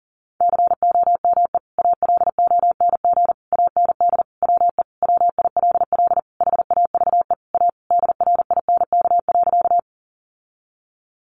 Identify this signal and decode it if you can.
Morse code, heard as a keyed tone, about 40 wpm, 700 Hz. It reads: COME ALONG AND WE WILL HAVE A DRINK.
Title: Sentences from Top 1500 Words 40wpm